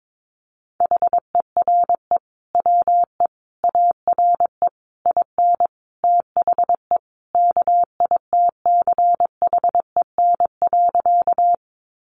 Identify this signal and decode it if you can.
Morse code, heard as a keyed tone, about 22 wpm, 700 Hz. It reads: HERE WE ARE IN THE KITCHEN.
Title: Sentences from Top 1400 Words 22wpm